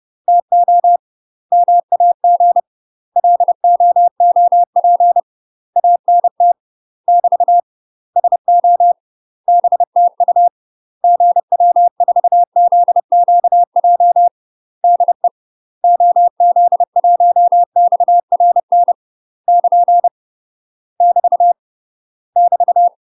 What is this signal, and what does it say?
Morse code, heard as a keyed tone, about 30 wpm, 700 Hz. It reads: TO MAG LOOP ANT <BT> SO BTU GW4ZQJ DE OZ1XRN <KN> <BT> <BT>
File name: UK MASTER QSOs TRAINING / 30WPM / UK MASTER QSOs 0000